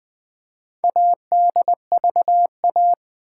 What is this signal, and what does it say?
Morse code, heard as a keyed tone, about 20 wpm, 700 Hz. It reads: ADVA
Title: Build A Word - 2 Letters At A Time - 8-Letter Words - 20wpm